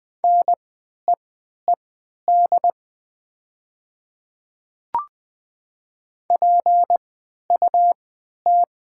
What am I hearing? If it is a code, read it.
Morse code, heard as a keyed tone, about 20 wpm, spaced as for 11 wpm, 700 Hz. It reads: NEED PUT
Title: 3x Character Spacing - Top 200 Words - 20wpm